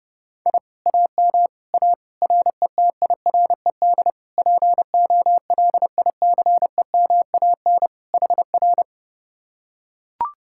Morse code, heard as a keyed tone, about 30 wpm, 700 Hz. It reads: I AM A RETIRED POLICEMAN HR